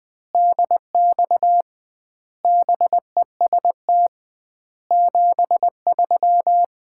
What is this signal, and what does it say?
Morse code, heard as a keyed tone, about 21 wpm, 700 Hz. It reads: DX BEST 73
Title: Common QSO Phrases - 2x Word-Spacing - 20wpm